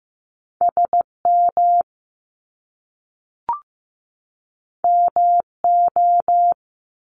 Morse code, heard as a keyed tone, about 15 wpm, 700 Hz. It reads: SM MO
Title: Top Two Letter Combinations in Words 15wpm